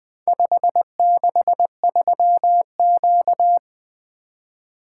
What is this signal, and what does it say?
Morse code, heard as a keyed tone, about 20 wpm, 700 Hz. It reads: 563Q
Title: Four Letter-Number Combinations - 20wpm